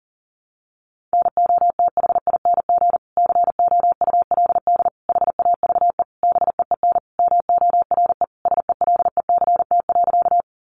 Morse code, heard as a keyed tone, about 40 wpm, 700 Hz. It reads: NOTHING COULD HAVE BEEN MORE SELECT.
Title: Sentences from Top 1300 Words 40wpm